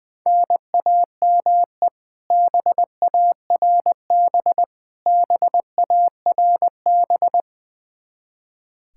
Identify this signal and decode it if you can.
Morse code, heard as a keyed tone, about 20 wpm, 700 Hz. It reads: NAME BARB BARB
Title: Practice QSO - Exchange 1 - Beginning - 20wpm